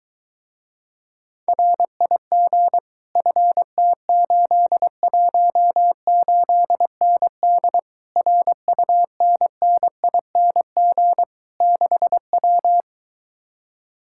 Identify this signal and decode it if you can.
Morse code, heard as a keyed tone, about 22 wpm, 700 Hz. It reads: RIG FT818ND RUNNING 6W